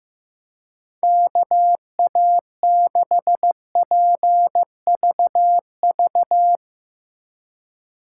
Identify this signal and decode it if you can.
Morse code, heard as a keyed tone, about 15 wpm, 700 Hz. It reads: KA6PVV
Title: USA Call Signs - 3x Character Spacing - 15wpm